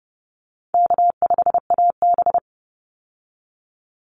Morse code, heard as a keyed tone, about 30 wpm, 700 Hz. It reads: K5AB
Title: Easy Call Signs Repeated 2x - 30wpm